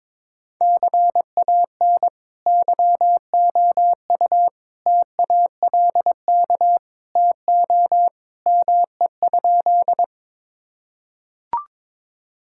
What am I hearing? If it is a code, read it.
Morse code, heard as a keyed tone, about 22 wpm, 700 Hz. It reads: CAN YOU TALK TO ME?
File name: practice / 22wpm / Sentences with 2-Word Phrases - 22wpm